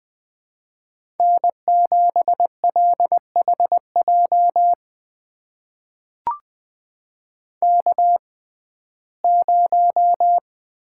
Morse code, heard as a keyed tone, about 21 wpm, spaced as for 15 wpm, 700 Hz. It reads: N7LHJ K0
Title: USA Call Signs - 5 Characters - 6x Character Spacing - 20wpm